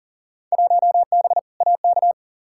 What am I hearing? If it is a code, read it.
Morse code, heard as a keyed tone, about 40 wpm, 700 Hz. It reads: 1B AK
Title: Field Day Exchanges Encore - 40wpm